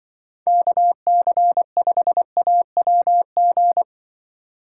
Morse code, heard as a keyed tone, about 24 wpm, 700 Hz. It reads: KC5AWG
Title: Speed Racing - USA Call Signs 20wpm